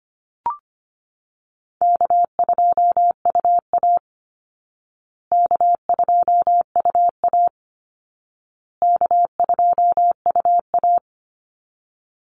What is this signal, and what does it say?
Morse code, heard as a keyed tone, about 25 wpm, 700 Hz. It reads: K2UA K2UA K2UA
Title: Easy Call Signs Repeated 3x - 25wpm